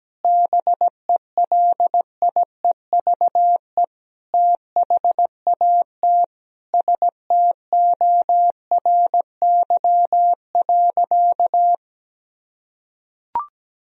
Morse code, heard as a keyed tone, about 17 wpm, 700 Hz. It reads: BELIEVE THAT STORY.
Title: Sentences from Top 1000 Words - Review Part 1 - 17wpm